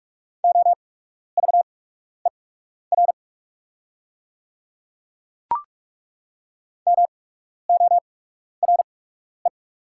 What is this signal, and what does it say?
Morse code, heard as a keyed tone, about 43 wpm, spaced as for 12 wpm, 700 Hz. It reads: OVER MORE
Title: 8x Character Spacing - Top 100 Words - 45wpm